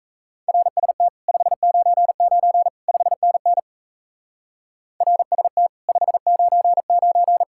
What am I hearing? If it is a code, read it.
Morse code, heard as a keyed tone, about 35 wpm, 700 Hz. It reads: RST 599 5NN RST 599